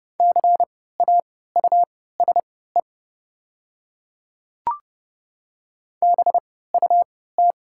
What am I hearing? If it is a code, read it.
Morse code, heard as a keyed tone, about 30 wpm, spaced as for 17 wpm, 700 Hz. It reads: CAUSE BUT
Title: 3x Character Spacing - Top 200 Words - 30wpm